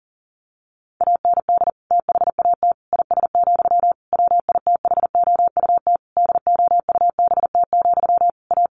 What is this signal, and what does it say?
Morse code, heard as a keyed tone, about 40 wpm, 700 Hz. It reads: AND THAT IS, WITHOUT DOUBT, A
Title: Sentences from Top 1500 Words 40wpm